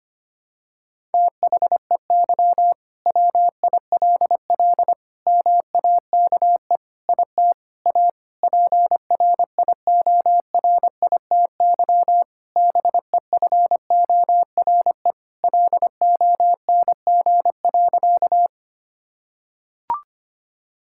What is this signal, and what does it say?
Morse code, heard as a keyed tone, about 25 wpm, 700 Hz. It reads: THEY WILL MAKE IT A PRIORITY BEFORE LONG.
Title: Sentences from Top 1600 Words 25wpm